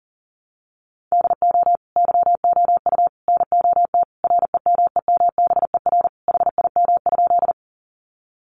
Morse code, heard as a keyed tone, about 40 wpm, 700 Hz. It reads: DO YOU NOT REMEMBER HIM?
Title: Sentences from Top 600 Words 40wpm